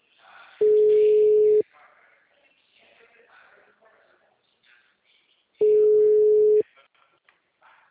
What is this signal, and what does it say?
Ringback tone (Europe, 425 Hz), with speech.
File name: out- -2026